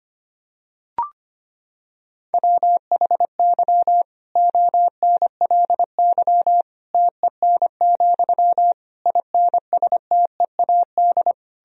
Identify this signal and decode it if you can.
Morse code, heard as a keyed tone, about 25 wpm, 700 Hz. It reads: WHY ONLY TEN, INSTEAD
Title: Sentences from Top 1000 Words - Review Part 4 - 25wpm